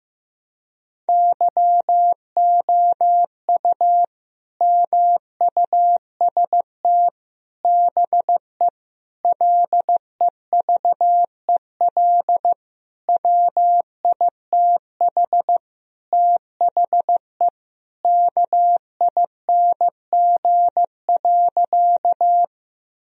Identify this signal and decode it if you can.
Morse code, heard as a keyed tone, about 15 wpm, 700 Hz. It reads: YOU MUST BE LEVEL WITH THE KING.
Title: Sentences from Top 1000 Words - Review Part 4 - 15wpm